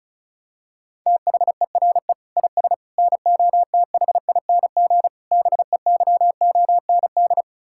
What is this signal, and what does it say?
Morse code, heard as a keyed tone, about 34 wpm, 700 Hz. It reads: THERE IS NOTHING BEYOND
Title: Sentences from Top 1000 Words - Review Part 6 - 35wpm